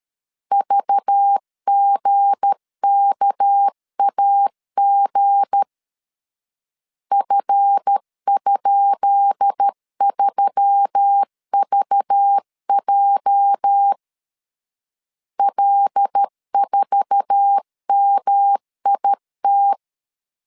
Morse code, heard as a keyed tone, about 13 wpm, 800 Hz. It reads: VGKAG F?3VJ L4MIT
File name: LEZIONI / 10 SIGNS / 10B INT 12wpm 2414